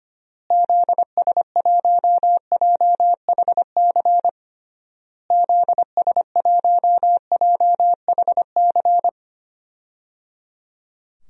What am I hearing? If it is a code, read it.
Morse code, heard as a keyed tone, about 25 wpm, 700 Hz. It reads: ZS1JHC ZS1JHC
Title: DXCC-100 Call Signs - 6 Characters - Repeated 3x - 25wpm